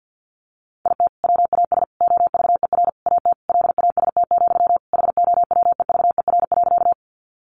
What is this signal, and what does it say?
Morse code, heard as a keyed tone, about 50 wpm, 700 Hz. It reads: IT WAS OVER AT LAST, HOWEVER.
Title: Sentences from Top 600 Words 50wpm